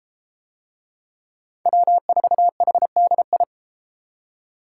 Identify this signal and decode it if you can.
Morse code, heard as a keyed tone, about 32 wpm, 700 Hz. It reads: W4HDI